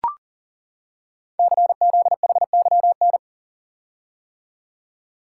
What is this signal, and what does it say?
Morse code, heard as a keyed tone, about 42 wpm, 690 Hz. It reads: CZHYN
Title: Code Groups - 5 Letters - 40wpm